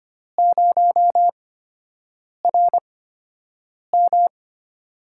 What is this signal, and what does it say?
Morse code, heard as a keyed tone, about 25 wpm, spaced as for 7 wpm, 700 Hz. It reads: 0RM
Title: USA Call Signs - 4 Characters - 8x Character Spacing - 25wpm